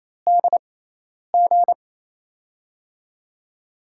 Morse code, heard as a keyed tone, about 28 wpm, spaced as for 10 wpm, 700 Hz. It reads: DG
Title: USA Call Signs - 6x Character Spacing - 28wpm